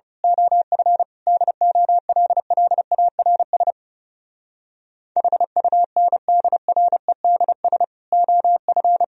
Morse code, heard as a keyed tone, about 28 wpm, 700 Hz. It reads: OF DOLLARS HUNDREDS OF
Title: Speed-Racing - Sets of 5 Words 30wpm